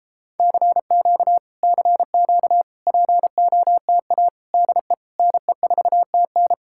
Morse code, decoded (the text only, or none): CQ CQ POTA DE NE4TN